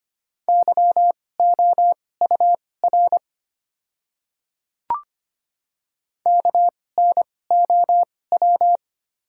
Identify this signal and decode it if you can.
Morse code, heard as a keyed tone, about 25 wpm, spaced as for 18 wpm, 700 Hz. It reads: YOUR KNOW
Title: 2x Character Spacing - Top 300 Words - 25wpm